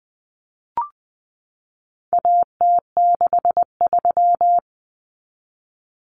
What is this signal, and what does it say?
Morse code, heard as a keyed tone, about 20 wpm, 700 Hz. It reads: AT63
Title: Common Cut Numbers - 4-Digit 20wpm